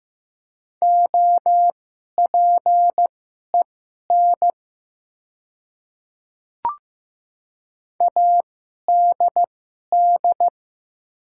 Morse code, heard as a keyed tone, about 15 wpm, spaced as for 11 wpm, 700 Hz. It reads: OPEN ADD
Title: 2x Character Spacing - Top 300 Words - 15wpm